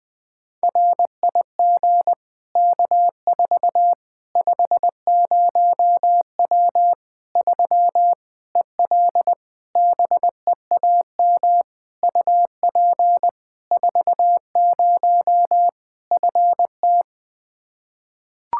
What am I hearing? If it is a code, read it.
Morse code, heard as a keyed tone, about 20 wpm, 700 Hz. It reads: RIG K4 50W 3 EL BEAM UP 40 FT
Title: Practice QSO - Exchange 2 - Info - 20wpm